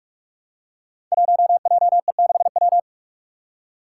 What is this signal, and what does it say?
Morse code, heard as a keyed tone, about 46 wpm, 700 Hz. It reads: 1JE6W
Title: Five Challenging Letter-Number Combinations 45wpm